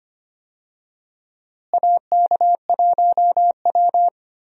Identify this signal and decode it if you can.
Morse code, heard as a keyed tone, about 25 wpm, 700 Hz. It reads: AK1W
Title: USA Call Signs - 4 Characters - Repeated 3x - 25wpm